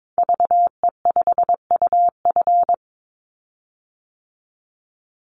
Morse code, heard as a keyed tone, about 22 wpm, 700 Hz. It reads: VE5UF
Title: Call Signs Repeated 2x - 22wpm